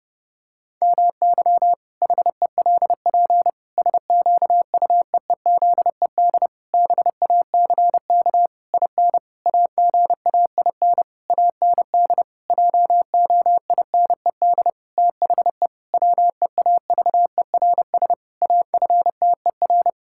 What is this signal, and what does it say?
Morse code, heard as a keyed tone, about 30 wpm, 700 Hz. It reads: MY HELP SQUEEZED BACK IN AGAIN AND JOINED THE WEAVERS AFTER
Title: Pangram Challenge v2 - 30wpm